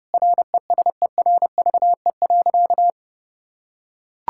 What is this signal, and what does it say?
Morse code, heard as a keyed tone, about 30 wpm, 700 Hz. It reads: RESERVE.
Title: Sentences from Top 1000 Words - Review Part 6 - 30wpm